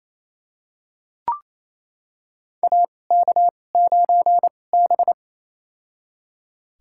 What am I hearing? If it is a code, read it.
Morse code, heard as a keyed tone, about 28 wpm, spaced as for 20 wpm, 700 Hz. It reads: AK9B
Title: USA Call Signs - 4 Characters - 2x Character Spacing - 28wpm